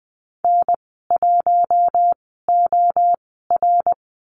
Morse code, heard as a keyed tone, about 20 wpm, spaced as for 15 wpm, 700 Hz. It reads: N1OR
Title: 062 Lesson 11g - Letter h - Call Signs 20wpm